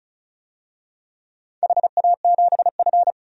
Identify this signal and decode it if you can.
Morse code, heard as a keyed tone, about 36 wpm, 700 Hz. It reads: HA7F
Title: DXCC-100 Call Signs - 4 Characters - 35wpm